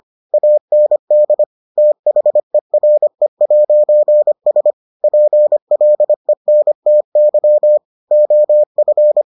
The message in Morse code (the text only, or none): AND THERE'S PLENTY OF